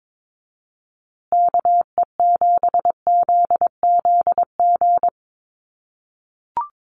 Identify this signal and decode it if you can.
Morse code, heard as a keyed tone, about 22 wpm, 700 Hz. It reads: KE7ZZG
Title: USA Call Signs Repeated 2x 22wpm